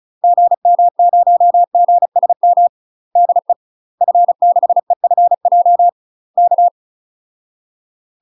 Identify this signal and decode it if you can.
Morse code, heard as a keyed tone, about 36 wpm, 700 Hz. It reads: GM0GSM DE F6EFJ K
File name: UK MASTER QSOs TRAINING / 35WPM / UK MASTER QSOs 0035